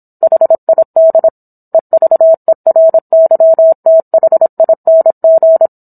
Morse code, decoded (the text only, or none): HID EVERYTHING